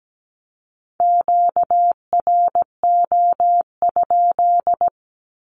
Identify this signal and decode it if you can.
Morse code, heard as a keyed tone, about 17 wpm, 700 Hz. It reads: QRO?
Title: Q-Codes with Spoken Meaning - 17wpm